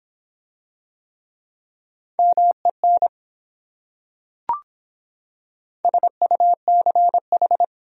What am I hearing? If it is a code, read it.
Morse code, heard as a keyed tone, about 26 wpm, 700 Hz. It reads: MEN SUCH